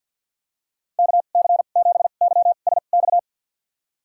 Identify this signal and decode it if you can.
Morse code, heard as a keyed tone, about 52 wpm, spaced as for 36 wpm, 700 Hz. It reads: KC7YSX
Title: USA Call Signs - 2x Character Spacing - 50wpm